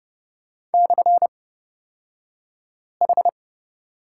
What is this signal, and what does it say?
Morse code, heard as a keyed tone, about 30 wpm, spaced as for 5 wpm, 700 Hz. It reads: /H
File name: Lesson 36d - BT - Two Characters 30wpm